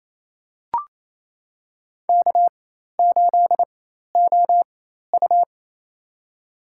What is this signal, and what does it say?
Morse code, heard as a keyed tone, about 28 wpm, spaced as for 13 wpm, 700 Hz. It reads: K8OU
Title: USA Call Signs - 4 Characters - 4x Character Spacing - 28wpm